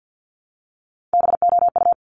Morse code, heard as a keyed tone, about 50 wpm, 700 Hz. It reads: BOU
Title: Top Three Letter Combinations in Words 50wpm